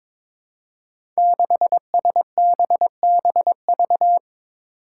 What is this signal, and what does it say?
Morse code, heard as a keyed tone, about 22 wpm, 700 Hz. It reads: 6SBBV
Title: Five Challenging Letter-Number Combinations 22wpm